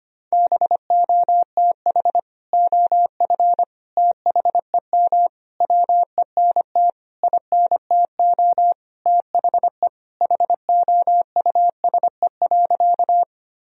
Morse code, heard as a keyed tone, about 25 wpm, 700 Hz. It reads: BOTH OF THEM WENT INTO THE HOUSE.
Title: Sentences from Top 1000 Words - Review Part 1 - 25wpm